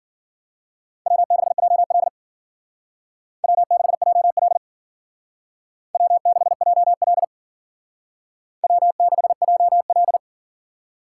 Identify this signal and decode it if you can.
Morse code, heard as a keyed tone, about 53 wpm, 700 Hz. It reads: W6JL W6JL W6JL W6JL